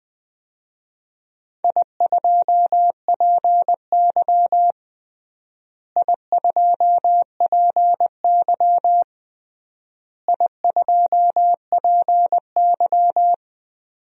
Morse code, heard as a keyed tone, about 20 wpm, 700 Hz. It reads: I2PY I2PY I2PY
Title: DXCC-100 Call Signs - 4 Characters - Repeated 3x - 20wpm